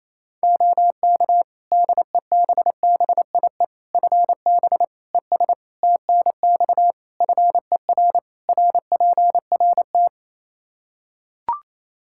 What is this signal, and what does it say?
Morse code, heard as a keyed tone, about 28 wpm, 700 Hz. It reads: OK DEBBIE FB ES TNX FER RPRT
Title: Practice QSO - Exchange 2 - Info - 28wpm